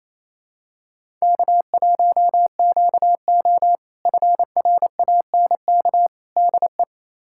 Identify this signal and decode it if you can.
Morse code, heard as a keyed tone, about 28 wpm, 700 Hz. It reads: K1QO FRANK DE
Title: K1USN SST QSO Practice - 28wpm